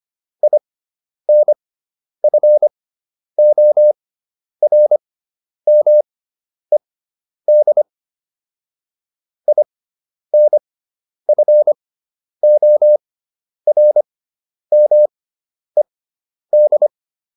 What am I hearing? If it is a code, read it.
Morse code, heard as a keyed tone, about 25 wpm, spaced as for 10 wpm, 600 Hz. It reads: INFORMED INFORMED